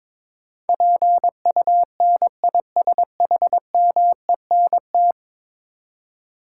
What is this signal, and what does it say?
Morse code, heard as a keyed tone, about 22 wpm, 700 Hz. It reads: PUNISHMENT
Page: morse-code-ninja-practice-22wpm